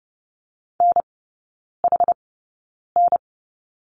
Morse code, heard as a keyed tone, about 30 wpm, spaced as for 9 wpm, 700 Hz. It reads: NHN
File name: course / 30:10wpm / 087 Lesson 15e - Number 5 - Three Characters 30wpm